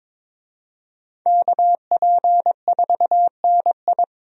Morse code, heard as a keyed tone, about 22 wpm, 700 Hz. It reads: KP4NI
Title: USA Call Signs - 5 Characters - Repeated 3x - 22wpm